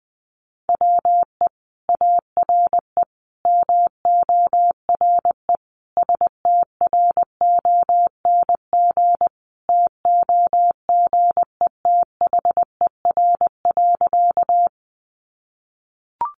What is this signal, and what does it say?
Morse code, heard as a keyed tone, about 20 wpm, 700 Hz. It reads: WE ARE MORE STRONG TOGETHER.
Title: Sentences from Top 500 Words 20wpm